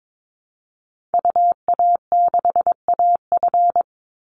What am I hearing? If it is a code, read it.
Morse code, heard as a keyed tone, about 22 wpm, 700 Hz. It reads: UA6AF
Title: Call Signs Repeated 3x - 22wpm